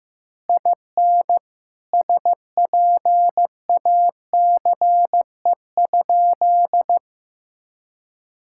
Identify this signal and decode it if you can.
Morse code, heard as a keyed tone, about 15 wpm, 700 Hz. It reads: IN SPACE?
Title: Sentences from Top 1000 Words - Review Part 5 - 15wpm